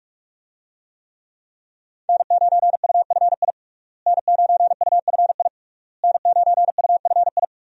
Morse code, heard as a keyed tone, about 46 wpm, 700 Hz. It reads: N9UFI N9UFI N9UFI
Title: USA Call Signs - 5 Characters - Repeated 3x - 45wpm